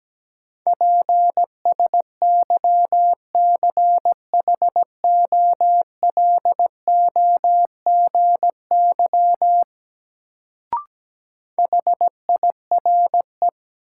Morse code, heard as a keyed tone, about 17 wpm, 700 Hz. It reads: PSYCHOLOGY HIRE
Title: Sentences from Top 1400 Words - New Word Drill Alt - 17wpm